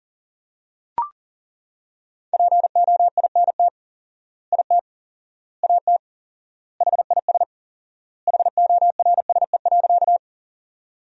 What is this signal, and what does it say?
Morse code, heard as a keyed tone, about 40 wpm, 700 Hz. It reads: POINT IT AT HIS HORSE.
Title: 4x Word-Spacing - Sentences from Top 300 Words - 40wpm